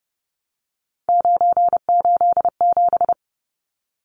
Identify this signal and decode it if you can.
Morse code, heard as a keyed tone, about 30 wpm, 700 Hz. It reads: 987
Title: Numbers - 3-Digit 30wpm